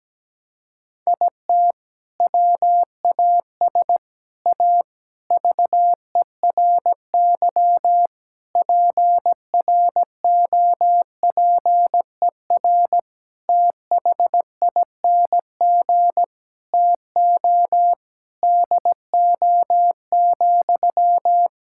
Morse code, heard as a keyed tone, about 17 wpm, 700 Hz. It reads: IT WAS A VERY PROPER THING TO DO,
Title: Sentences from Top 1000 Words - Review Part 5 - 17wpm